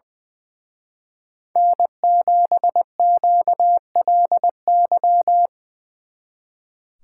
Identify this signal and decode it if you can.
Morse code, heard as a keyed tone, about 24 wpm, 700 Hz. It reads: N7QLY
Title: Speed Racing - USA 1x3 Call Signs 20wpm